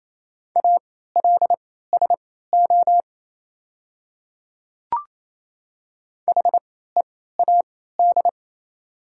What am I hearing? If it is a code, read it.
Morse code, heard as a keyed tone, about 28 wpm, spaced as for 16 wpm, 700 Hz. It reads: ALSO HEAD
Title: 3x Character Spacing - Top 300 Words - 28wpm